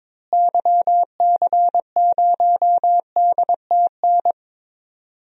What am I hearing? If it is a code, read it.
Morse code, heard as a keyed tone, about 22 wpm, 700 Hz. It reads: YC0DTN
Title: DXCC-100 Call Signs - 6 Characters - Repeated 2x - 22wpm